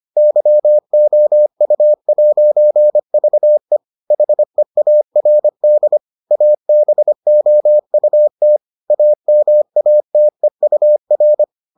Morse code, heard as a keyed tone, about 25 wpm, 600 Hz. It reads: YOU'VE HEARD ABOUT AMATEUR